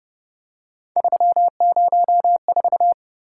Morse code, heard as a keyed tone, about 31 wpm, 700 Hz. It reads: 304
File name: Speed-Racing - Three Letter-Number Combinations - 25wpm